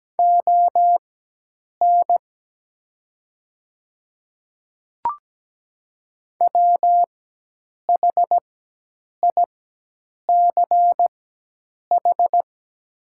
Morse code, heard as a keyed tone, about 17 wpm, spaced as for 8 wpm, 700 Hz. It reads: ON WHICH
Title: 4x Character Spacing - Top 100 Words - 17wpm